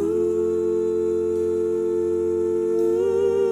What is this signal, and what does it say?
Dial tone (North America), 350 + 440 Hz, with noise.
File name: wrct-log hi- 09